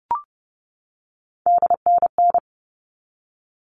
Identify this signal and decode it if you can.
Morse code, heard as a keyed tone, about 30 wpm, 700 Hz. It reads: DNN
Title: All Cut Numbers - 3-Digit 30wpm